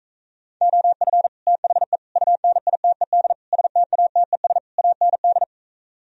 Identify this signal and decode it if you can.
Morse code, heard as a keyed tone, about 45 wpm, 700 Hz. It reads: OF THE UNITED STATES AND